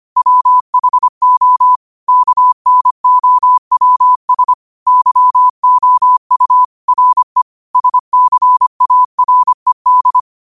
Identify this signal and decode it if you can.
Morse code, heard as a keyed tone, about 25 wpm, 1000 Hz. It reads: WHO KNOWS YOU RE SCARED
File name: 25wpm